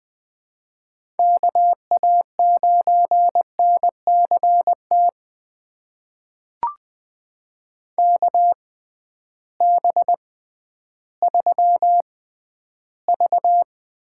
Morse code, heard as a keyed tone, about 20 wpm, spaced as for 12 wpm, 700 Hz. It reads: KA9NCT KB3V
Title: USA Call Signs - 6 Characters - 6x Character Spacing - 20wpm